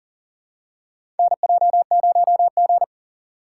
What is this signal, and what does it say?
Morse code, heard as a keyed tone, about 40 wpm, 700 Hz. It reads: NJ0G
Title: USA Call Signs - 4 Characters - 3x Character Spacing - 40wpm